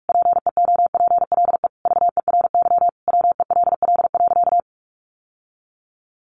Morse code, heard as a keyed tone, about 44 wpm, 700 Hz. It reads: PEOPLE VERY WELL.
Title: Sentences from Top 1300 Words 45wpm